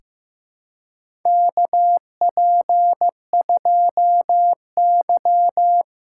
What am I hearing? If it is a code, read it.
Morse code, heard as a keyed tone, about 15 wpm, 700 Hz. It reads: KP2Y
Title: USA Callsign DE Callsign - 4 Characters - 15wpm